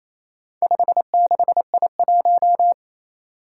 Morse code, heard as a keyed tone, about 28 wpm, 700 Hz. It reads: 56I1
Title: Four Challenging Letter-Number Combinations 28wpm